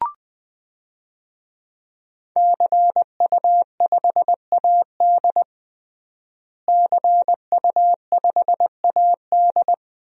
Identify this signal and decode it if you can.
Morse code, heard as a keyed tone, about 20 wpm, 700 Hz. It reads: CU5AD CU5AD
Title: DXCC-100 Call Signs - 5 Characters - Repeated 2x - 20wpm